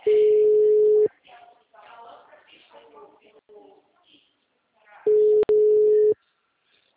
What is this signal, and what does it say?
Ringback tone (Europe, 425 Hz), with speech.
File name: out- -2009- - - .1964